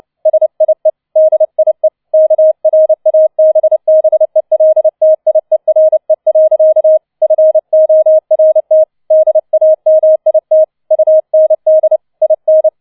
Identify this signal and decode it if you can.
Morse code, heard as a keyed tone, about 29 wpm, 630 Hz. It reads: SIE DIE KRABBELTIERE. FORT DAMIT UND IN